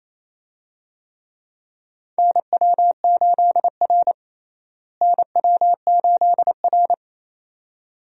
Morse code, heard as a keyed tone, about 27 wpm, 700 Hz. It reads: NW8R NW8R
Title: USA Call Signs - 4 Characters - Repeated 2x - 28wpm